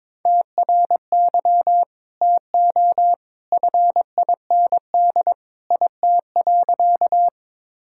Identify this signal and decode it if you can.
Morse code, heard as a keyed tone, about 22 wpm, 700 Hz. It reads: TRY TO FIND IT.
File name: practice / 22wpm / Sentences with 2-Word Phrases - 22wpm